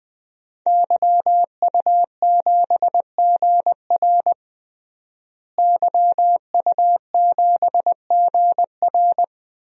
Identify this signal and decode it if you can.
Morse code, heard as a keyed tone, about 20 wpm, 700 Hz. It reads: YU7GR YU7GR
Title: DXCC-100 Call Signs - 5 Characters - Repeated 2x - 20wpm